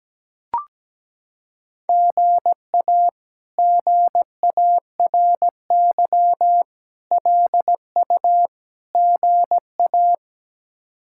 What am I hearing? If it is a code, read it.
Morse code, heard as a keyed tone, about 17 wpm, 700 Hz. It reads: GA GARY LU GA
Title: K1USN SST QSO Practice - 17wpm